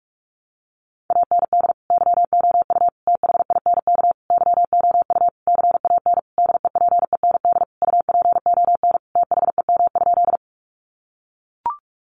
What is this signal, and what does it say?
Morse code, heard as a keyed tone, about 44 wpm, 700 Hz. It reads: AND YOU THINK YOU CAN DEPEND UPON THEM?
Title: Sentences from Top 1300 Words 45wpm